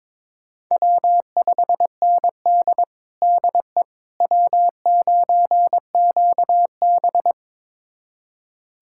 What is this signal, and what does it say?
Morse code, heard as a keyed tone, about 22 wpm, 700 Hz. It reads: W5ND DE W9QB
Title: USA Callsign DE Callsign - 4 Characters - 22wpm